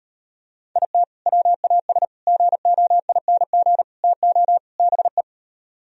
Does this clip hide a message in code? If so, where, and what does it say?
Yes: Morse code from 0.8 to 5.2 s, heard as a keyed tone, about 38 wpm, 700 Hz. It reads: IT WAS GOING TO BE